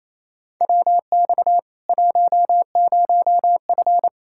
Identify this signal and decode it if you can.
Morse code, heard as a keyed tone, about 27 wpm, 700 Hz. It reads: WX 10F
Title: Practice QSO - Exchange 2 - Info - 28wpm